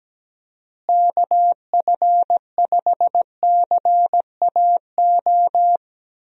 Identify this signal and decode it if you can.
Morse code, heard as a keyed tone, about 17 wpm, 700 Hz. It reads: KF5CAO
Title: USA Call Signs - 6 Characters - Repeated 3x - 17wpm